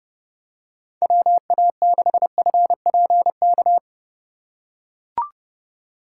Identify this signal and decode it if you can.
Morse code, heard as a keyed tone, about 31 wpm, 700 Hz. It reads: WA6FPK
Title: USA Call Signs - 6 Characters - 4x Character Spacing - 30wpm